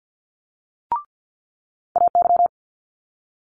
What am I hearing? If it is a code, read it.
Morse code, heard as a keyed tone, about 50 wpm, 700 Hz. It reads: AY
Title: Multi-letter Phonemes Quick Fire 50wpm